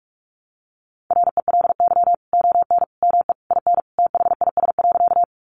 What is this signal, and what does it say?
Morse code, heard as a keyed tone, about 44 wpm, 700 Hz. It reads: RELY ON ME IN THIS.
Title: Sentences from Top 1200 Words 45wpm